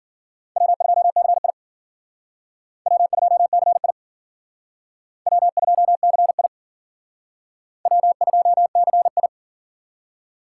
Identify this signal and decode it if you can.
Morse code, heard as a keyed tone, about 53 wpm, 700 Hz. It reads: W2CI W2CI W2CI W2CI